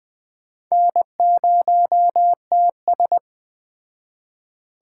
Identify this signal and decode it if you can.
Morse code, heard as a keyed tone, about 21 wpm, 700 Hz. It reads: N0TS
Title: Speed Racing - USA 1x2 Call Signs 17wpm